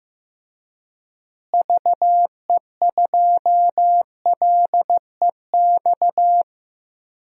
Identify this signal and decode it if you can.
Morse code, heard as a keyed tone, about 15 wpm, 700 Hz. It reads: VE2LEX
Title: DXCC-100 Call Signs - 6 Characters - Repeated 2x - 15wpm